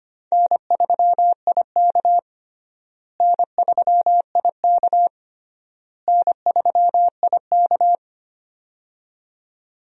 Morse code, heard as a keyed tone, about 25 wpm, 700 Hz. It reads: N3IK N3IK N3IK
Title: USA Call Signs - 4 Characters - Repeated 3x - 25wpm